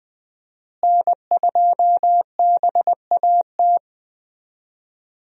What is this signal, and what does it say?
Morse code, heard as a keyed tone, about 20 wpm, 700 Hz. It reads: N2BAT
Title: USA Call Signs - 5 Characters - Repeated 2x - 20wpm